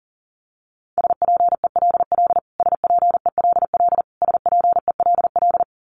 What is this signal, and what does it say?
Morse code, heard as a keyed tone, about 40 wpm, 700 Hz. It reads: SPELL SPELL SPELL
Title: Top 200 Words Repeated 40wpm